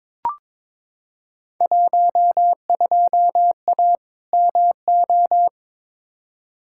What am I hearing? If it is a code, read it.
Morse code, heard as a keyed tone, about 22 wpm, 700 Hz. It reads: 12A MO
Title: Field Day Exchanges Encore - 22wpm